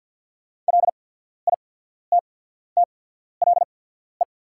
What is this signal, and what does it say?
Morse code, heard as a keyed tone, about 48 wpm, spaced as for 14 wpm, 690 Hz. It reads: LITTLE
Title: 8x Character Spacing - Top 200 Words - 50wpm